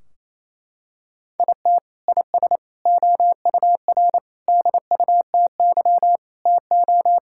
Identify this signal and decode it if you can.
Morse code, heard as a keyed tone, about 28 wpm, 700 Hz. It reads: IT IS OUR DUTY TO
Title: Sentences from Top 1600 Words 28wpm